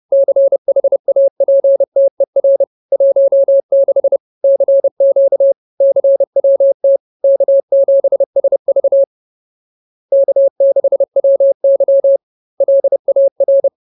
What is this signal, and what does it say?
Morse code, heard as a keyed tone, about 30 wpm, 550 Hz. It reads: CHAPTER 16 CQ CWT K7SV K6WY LAR
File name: CWT-Chapter-16